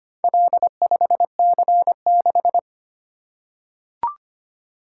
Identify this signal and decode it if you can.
Morse code, heard as a keyed tone, about 25 wpm, 700 Hz. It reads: L5C6
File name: practice / 25wpm / Four Letter-Number Combinations - 25wpm